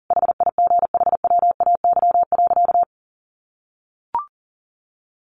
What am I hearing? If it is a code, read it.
Morse code, heard as a keyed tone, about 40 wpm, 700 Hz. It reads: HIGHWAY.
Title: Sentences from Top 1400 Words 40wpm